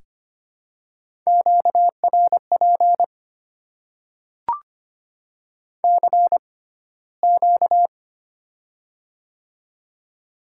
Morse code, heard as a keyed tone, about 25 wpm, spaced as for 15 wpm, 700 Hz. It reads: QRP CQ
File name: Top 90 QSO Elements - 6x Character Spacing - 25wpm